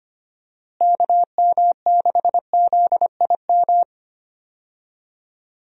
Morse code, heard as a keyed tone, about 25 wpm, 700 Hz. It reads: KM6ZIM
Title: USA Call Signs - 6 Characters - Repeated 2x - 25wpm